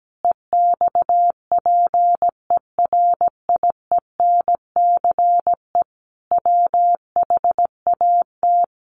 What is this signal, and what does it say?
Morse code, heard as a keyed tone, about 17 wpm, 700 Hz. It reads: EXPERIENCE WHAT
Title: Sentences from Top 1100 Words 17wpm